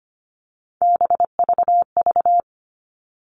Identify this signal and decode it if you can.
Morse code, heard as a keyed tone, about 25 wpm, 700 Hz. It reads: BVV
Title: All Cut Numbers - 3-Digit 25wpm